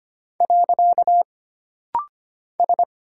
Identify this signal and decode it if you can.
Morse code, heard as a keyed tone, about 25 wpm, 700 Hz. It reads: . S
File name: Single Character (Letter:Number:Punctuation:Prosign) - Rapid-Fire - 25wpm